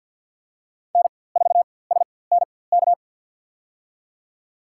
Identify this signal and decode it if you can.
Morse code, heard as a keyed tone, about 50 wpm, spaced as for 23 wpm, 700 Hz. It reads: N4SNK
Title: USA Call Signs - 5 Characters - 4x Character Spacing - 50wpm